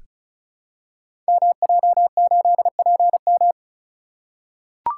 Morse code, heard as a keyed tone, about 34 wpm, 700 Hz. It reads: MJ8PM
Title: Code Groups - 5 Letter:Numbers - 35wpm